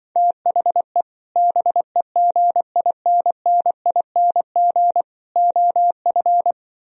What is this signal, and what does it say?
Morse code, heard as a keyed tone, about 25 wpm, 700 Hz. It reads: THE BEGINNING OF
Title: Speed-Racing - Sets of 3 Words 20wpm